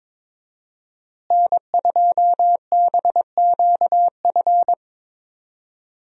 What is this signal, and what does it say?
Morse code, heard as a keyed tone, about 22 wpm, 700 Hz. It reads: N2BQF
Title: USA Call Signs - 5 Characters - Repeated 3x - 22wpm